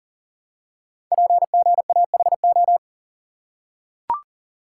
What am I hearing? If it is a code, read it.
Morse code, heard as a keyed tone, about 40 wpm, 700 Hz. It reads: PGAHO
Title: Code Groups - 5 Letters - 40wpm